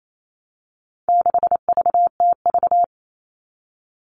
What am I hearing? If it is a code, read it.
Morse code, heard as a keyed tone, about 28 wpm, 700 Hz. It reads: 6VTV
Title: All Cut Numbers - 4-Digit 28wpm